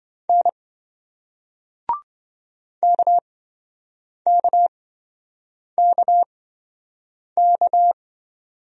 Morse code, heard as a keyed tone, about 28 wpm, spaced as for 7 wpm, 700 Hz. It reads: N KKKK